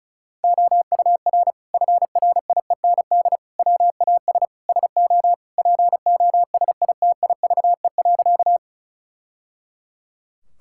Morse code, heard as a keyed tone, about 36 wpm, 700 Hz. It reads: OUR FRIEND WAS SO POSITIVE.
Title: Sentences from Top 1000 Words - Review Part 5 - 35wpm